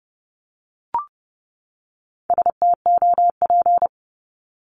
Morse code, heard as a keyed tone, about 30 wpm, 700 Hz. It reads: STOP
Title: Top 500 Words 30wpm